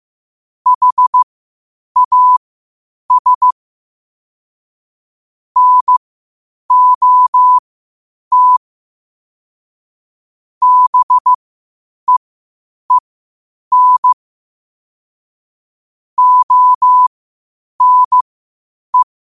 Morse code, heard as a keyed tone, about 15 wpm, spaced as for 8 wpm, 1000 Hz. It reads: HAS NOT BEEN ONE
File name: qotdfiles / 8wpm